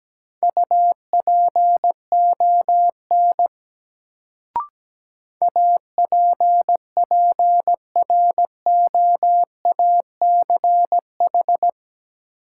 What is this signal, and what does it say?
Morse code, heard as a keyed tone, about 17 wpm, 700 Hz. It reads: UPON APPROACH
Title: Sentences from Top 1000 Words - Review Part 6 - New Word Drill - 17wpm